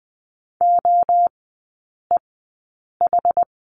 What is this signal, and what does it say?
Morse code, heard as a keyed tone, about 20 wpm, spaced as for 8 wpm, 700 Hz. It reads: OEH
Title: 060 Lesson 11e - Letter h - Three Characters 20wpm